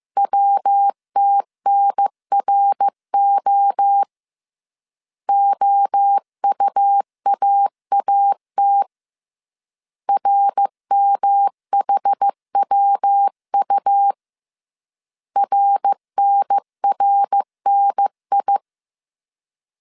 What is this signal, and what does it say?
Morse code, heard as a keyed tone, about 15 wpm, 800 Hz. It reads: WTNRO OUAAT RMHWU RNRNI